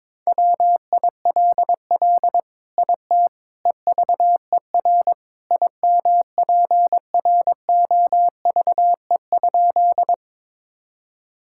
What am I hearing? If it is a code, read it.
Morse code, heard as a keyed tone, about 22 wpm, 700 Hz. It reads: WILL IT EVER IMPROVE?
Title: Sentences from Top 1000 Words - Review Part 5 - 22wpm